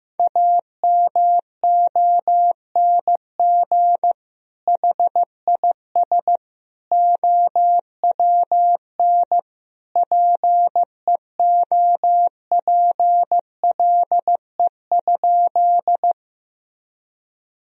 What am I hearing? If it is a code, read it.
Morse code, heard as a keyed tone, about 16 wpm, 700 Hz. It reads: AMONG HIS OWN PEOPLE?
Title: Sentences from Top 1000 Words - Review Part 3 - 15wpm